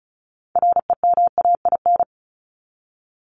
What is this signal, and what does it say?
Morse code, heard as a keyed tone, about 34 wpm, 700 Hz. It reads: REMAIN
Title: ES:S Suffix - Top 100 Words with Base Words 35wpm